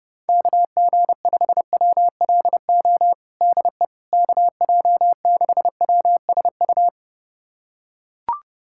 Morse code, heard as a keyed tone, about 30 wpm, 700 Hz. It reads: KG5WLO DE KJ6WSU
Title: USA Callsign DE Callsign 30wpm